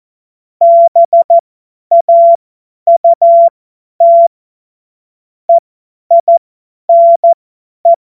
Morse code, heard as a keyed tone, about 14 wpm, spaced as for 10 wpm, 680 Hz. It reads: BAUT EINE
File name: Morsekurs / Witze / Witz 31. Mai 2022 50 F70